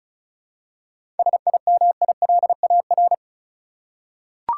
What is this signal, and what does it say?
Morse code, heard as a keyed tone, about 34 wpm, 700 Hz. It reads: SIMILAR
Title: Seven Letter Words - 35wpm